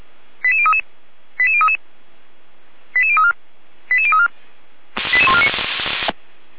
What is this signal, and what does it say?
Five-tone selective call (ZVEI), heard as quick runs of five tones, tones of about 70 ms. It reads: 88033 88033 88034 88034 88149